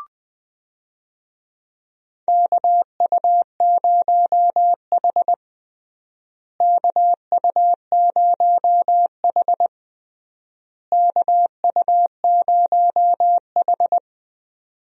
Morse code, heard as a keyed tone, about 21 wpm, 700 Hz. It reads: KU0H KU0H KU0H
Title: USA Call Signs - 4 Characters - Repeated 3x - 20wpm